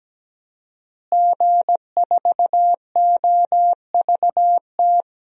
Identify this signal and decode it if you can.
Morse code, heard as a keyed tone, about 17 wpm, 700 Hz. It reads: G4OVT
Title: DXCC-100 Call Signs - 5 Characters - Repeated 3x - 17wpm